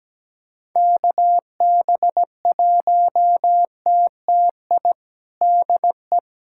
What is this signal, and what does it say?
Morse code, heard as a keyed tone, about 17 wpm, 700 Hz. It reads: KB1TTI DE
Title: USA Callsign DE Callsign 17wpm